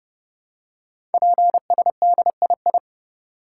Morse code, heard as a keyed tone, about 30 wpm, 700 Hz. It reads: PSDII
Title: Code Groups - 5 Letter:Numbers - 30wpm